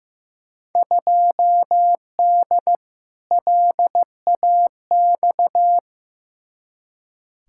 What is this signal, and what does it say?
Morse code, heard as a keyed tone, about 15 wpm, 700 Hz. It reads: 2D LAX
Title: Field Day Exchanges - 15wpm